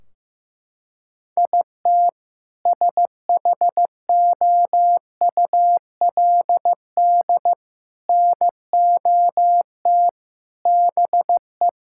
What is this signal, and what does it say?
Morse code, heard as a keyed tone, about 15 wpm, 700 Hz. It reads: IT SHOULD NOT BE
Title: Sentences from Top 1600 Words 15wpm